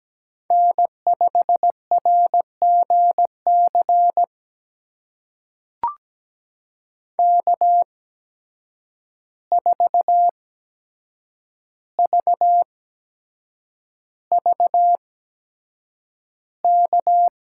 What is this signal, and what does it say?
Morse code, heard as a keyed tone, about 17 wpm, spaced as for 7 wpm, 700 Hz. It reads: N5RGC K4VVK
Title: USA Call Signs - 5 Characters - 8x Character Spacing - 17wpm